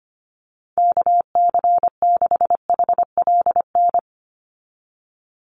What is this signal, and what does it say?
Morse code, heard as a keyed tone, about 25 wpm, 700 Hz. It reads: KC6HLN
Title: USA Call Signs 25wpm